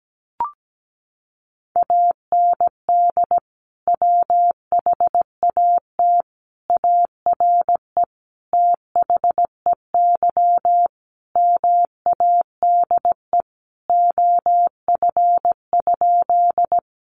Morse code, heard as a keyed tone, about 17 wpm, 700 Hz. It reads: AND WHAT ARE THEY MADE OF?
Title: Sentences from Top 200 Words 17wpm